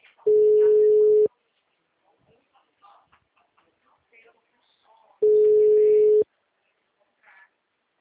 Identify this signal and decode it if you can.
Ringback tone (Europe, 425 Hz), with speech.
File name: out- -2006- - - .62834